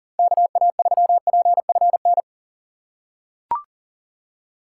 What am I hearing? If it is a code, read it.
Morse code, heard as a keyed tone, about 40 wpm, 690 Hz. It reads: KA3PFN
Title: USA Call Signs - 2x Character Spacing - 40wpm